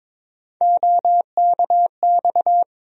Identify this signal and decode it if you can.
Morse code, heard as a keyed tone, about 22 wpm, 700 Hz. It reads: OKX
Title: USA Call Sign Postfixes - 3 Letters 22wpm